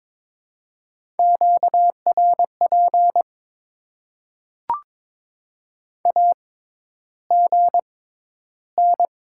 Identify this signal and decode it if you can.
Morse code, heard as a keyed tone, about 22 wpm, spaced as for 11 wpm, 700 Hz. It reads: QRP AGN
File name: Top 90 QSO Elements - 6x Character Spacing - 22wpm